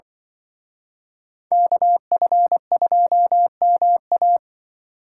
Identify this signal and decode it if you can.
Morse code, heard as a keyed tone, about 24 wpm, 700 Hz. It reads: KF2MA
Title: Speed Racing - USA 2x2 Call Signs 20wpm